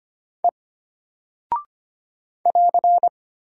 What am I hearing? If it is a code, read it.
Morse code, heard as a keyed tone, about 25 wpm, 700 Hz. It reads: E <AR>
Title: Single Character (Letter:Number:Punctuation:Prosign) Alt 25wpm